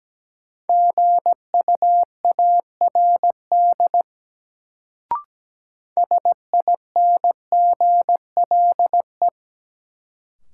Morse code, heard as a keyed tone, about 17 wpm, 700 Hz. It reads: GUARD SINGLE
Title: Sentences from Top 700 Words - New Word Drill - 17wpm